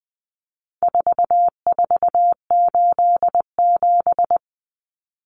Morse code, heard as a keyed tone, about 19 wpm, 700 Hz. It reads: 4487
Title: Numbers - 4-Digit 20wpm